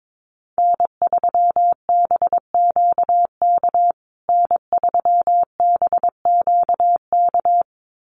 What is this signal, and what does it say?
Morse code, heard as a keyed tone, about 22 wpm, 700 Hz. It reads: N3BQK N3BQK
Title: USA Call Signs Repeated 2x 22wpm